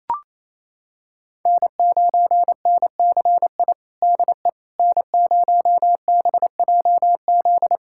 Morse code, heard as a keyed tone, about 27 wpm, 700 Hz. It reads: N9NCI DE N0BJZ
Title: USA Callsign DE Callsign - 5 Characters - 28wpm